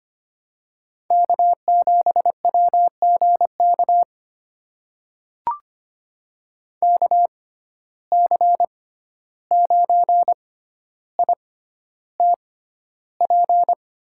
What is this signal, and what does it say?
Morse code, heard as a keyed tone, about 25 wpm, spaced as for 12 wpm, 700 Hz. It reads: K7WGK KC9ITP
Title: USA Call Signs - 6x Character Spacing - 25wpm